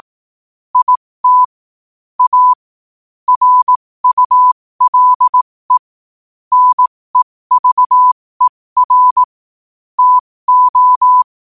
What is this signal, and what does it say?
Morse code, heard as a keyed tone, about 18 wpm, spaced as for 15 wpm, 1000 Hz. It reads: IT A RULE NEVER TO
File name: qotdfiles / 15wpm